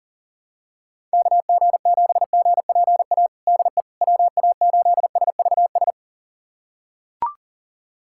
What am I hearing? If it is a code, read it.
Morse code, heard as a keyed tone, about 40 wpm, 700 Hz. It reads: KG7GPA DE WA8SVS
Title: USA Callsign DE Callsign 40wpm